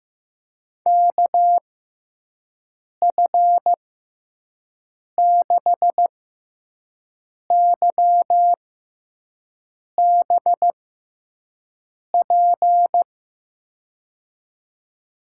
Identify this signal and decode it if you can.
Morse code, heard as a keyed tone, about 16 wpm, spaced as for 5 wpm, 700 Hz. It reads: KF6YBP
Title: USA Call Signs - 6 Characters - 6x Character Spacing - 15wpm